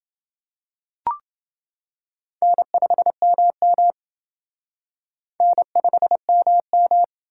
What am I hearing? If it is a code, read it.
Morse code, heard as a keyed tone, about 28 wpm, 700 Hz. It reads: N5MM N5MM